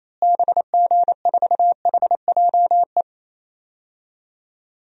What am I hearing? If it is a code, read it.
Morse code, heard as a keyed tone, about 28 wpm, 700 Hz. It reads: BG4HJE
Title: DXCC-100 Call Signs - 6 Characters - Repeated 3x - 28wpm